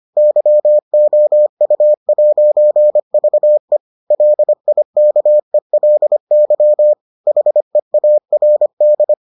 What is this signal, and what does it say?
Morse code, heard as a keyed tone, about 25 wpm, 600 Hz. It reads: YOU'VE LIKELY HEARD